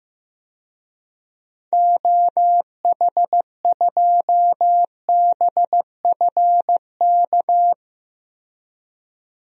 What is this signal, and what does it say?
Morse code, heard as a keyed tone, about 15 wpm, 700 Hz. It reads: OH2BFK
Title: DXCC-100 Call Signs - 6 Characters - 15wpm